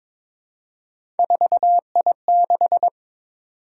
Morse code, heard as a keyed tone, about 22 wpm, 700 Hz. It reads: 4I6
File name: practice / 22wpm / Three Letter-Number Combinations - 22wpm